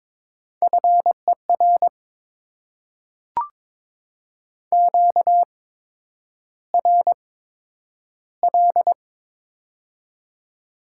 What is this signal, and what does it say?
Morse code, heard as a keyed tone, about 22 wpm, spaced as for 9 wpm, 700 Hz. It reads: FER QRL
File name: Top 20 QSO Elements - 8x Character Spacing - 22wpm